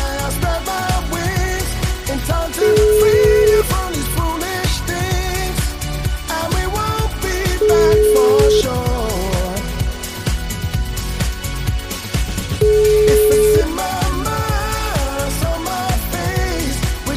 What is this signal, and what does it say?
Ringback tone (Europe, 425 Hz), with noise.